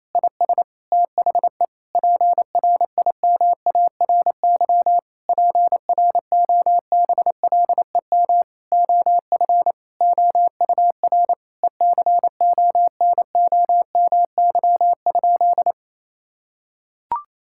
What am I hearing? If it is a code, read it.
Morse code, heard as a keyed tone, about 28 wpm, 700 Hz. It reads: IS THE PRIMARY PROBLEM OF OUR ECONOMY?
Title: Sentences from Top 1000 Words - Review Part 5 - 28wpm